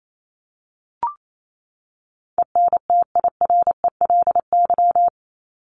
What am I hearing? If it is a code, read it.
Morse code, heard as a keyed tone, about 28 wpm, 700 Hz. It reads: ENTIRELY
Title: LY Suffix - Top 100 Words with Base Words 28wpm